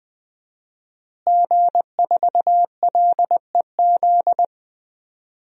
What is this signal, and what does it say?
Morse code, heard as a keyed tone, about 20 wpm, 700 Hz. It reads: G4LEZ
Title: DXCC-100 Call Signs - 5 Characters - Repeated 2x - 20wpm